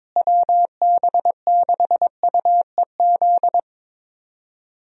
Morse code, heard as a keyed tone, about 22 wpm, 700 Hz. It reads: WB6UEZ
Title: USA Call Signs - 6 Characters - Repeated 2x - 22wpm